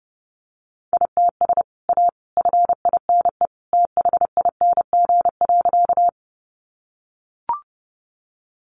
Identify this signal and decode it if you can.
Morse code, heard as a keyed tone, about 30 wpm, 700 Hz. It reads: ITS A FINE THING.
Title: Sentences from Top 600 Words 30wpm